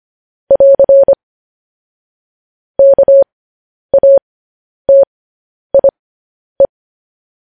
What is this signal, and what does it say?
Morse code, heard as a keyed tone, about 25 wpm, spaced as for 10 wpm, 550 Hz. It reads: <AR> KATIE